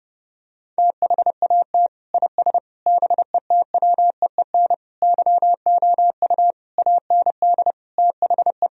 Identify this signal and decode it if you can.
Morse code, heard as a keyed tone, about 30 wpm, 700 Hz. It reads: THAT IS BETWEEN YOU AND THE
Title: Sentences from Top 1000 Words - Review Part 4 - 30wpm